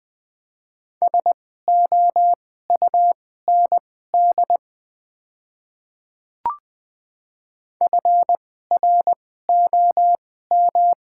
Morse code, heard as a keyed tone, about 20 wpm, spaced as for 14 wpm, 700 Hz. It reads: SOUND FROM
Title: 2x Character Spacing - Top 100 Words - 20wpm